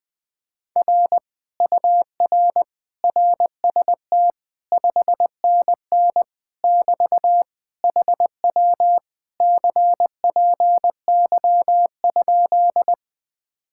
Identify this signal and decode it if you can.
Morse code, heard as a keyed tone, about 20 wpm, 700 Hz. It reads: R UR RST 5NN <BT> HW CPY?
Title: Speed-Racing Common QSO Phrases - 20wpm